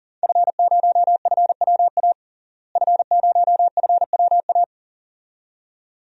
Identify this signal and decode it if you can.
Morse code, heard as a keyed tone, about 40 wpm, 700 Hz. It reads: F0FWA F0FWA
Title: DXCC-100 Call Signs - 5 Characters - Repeated 2x - 40wpm